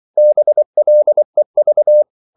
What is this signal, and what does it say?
Morse code, heard as a keyed tone, about 24 wpm, 600 Hz. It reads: BLEV
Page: Ferie 24 wpm